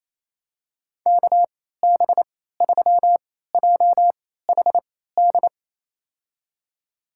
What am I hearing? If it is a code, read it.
Morse code, heard as a keyed tone, about 28 wpm, spaced as for 16 wpm, 700 Hz. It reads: KB3JHD
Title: USA Call Signs - 6 Characters - 3x Character Spacing - 28wpm